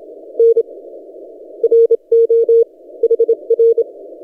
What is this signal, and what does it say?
Morse code, heard as a keyed tone, about 27 wpm, 460 Hz. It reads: N RO HR
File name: posts / t32rc-30cw